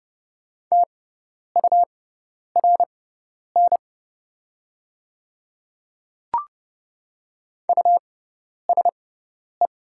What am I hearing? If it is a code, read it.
Morse code, heard as a keyed tone, about 31 wpm, spaced as for 10 wpm, 700 Hz. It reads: TURN USE
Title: 6x Character Spacing - Top 200 Words - 30wpm